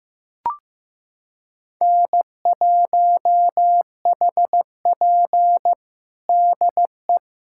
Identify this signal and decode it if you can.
Morse code, heard as a keyed tone, about 15 wpm, 700 Hz. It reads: N1HP DE
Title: USA Callsign DE Callsign - 4 Characters - 15wpm